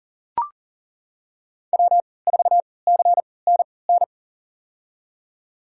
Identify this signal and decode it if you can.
Morse code, heard as a keyed tone, about 37 wpm, spaced as for 22 wpm, 700 Hz. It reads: W4CNN
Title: USA Call Signs - 5 Characters - 3x Character Spacing - 40wpm